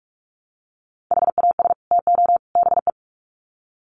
Morse code, heard as a keyed tone, about 44 wpm, 700 Hz. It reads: HAS TO BE
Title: Sets of 3 Words 45wpm